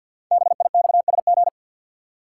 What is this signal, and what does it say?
Morse code, heard as a keyed tone, about 50 wpm, 700 Hz. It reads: BIXSG
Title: Code Groups - 5 Letters - 50wpm